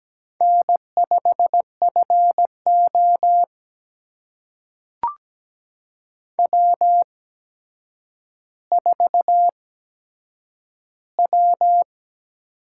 Morse code, heard as a keyed tone, about 17 wpm, spaced as for 8 wpm, 700 Hz. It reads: N5FO W4W
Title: USA Call Signs - 4 Characters - 8x Character Spacing - 17wpm